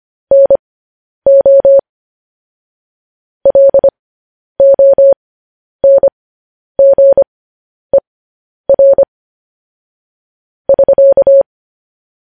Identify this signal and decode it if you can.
Morse code, heard as a keyed tone, about 25 wpm, spaced as for 10 wpm, 550 Hz. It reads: NO LONGER <SK>
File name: podcast / News headlines-1x-25x10